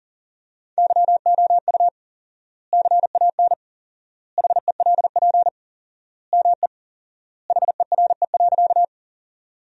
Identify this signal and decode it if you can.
Morse code, heard as a keyed tone, about 37 wpm, 700 Hz. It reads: YOU CAN HELP ME HERE.
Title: 4x Word-Spacing - Sentences from Top 200 Words - 40wpm